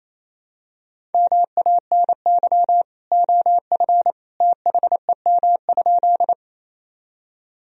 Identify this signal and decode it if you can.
Morse code, heard as a keyed tone, about 28 wpm, 700 Hz. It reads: MANY OF THEM?
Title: Sentences from Top 1000 Words - Review Part 1 - 28wpm